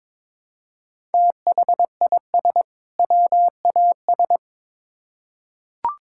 Morse code, heard as a keyed tone, about 22 wpm, 700 Hz. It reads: THIS WAS
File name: Build A Phrase - 2-Word Phrases - 22wpm